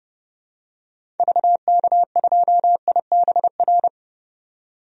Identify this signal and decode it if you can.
Morse code, heard as a keyed tone, about 32 wpm, 700 Hz. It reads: VK2IBR
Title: DXCC-100 Call Signs - 6 Characters - 30wpm